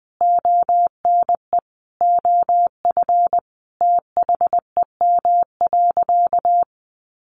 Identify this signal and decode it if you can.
Morse code, heard as a keyed tone, about 20 wpm, 700 Hz. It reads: ONE OF THEM.
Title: Additional Easy Phrases 20wpm